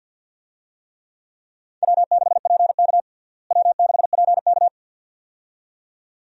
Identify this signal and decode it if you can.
Morse code, heard as a keyed tone, about 49 wpm, 700 Hz. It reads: W6PK W6PK
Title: USA Call Signs - 4 Characters - Repeated 2x - 50wpm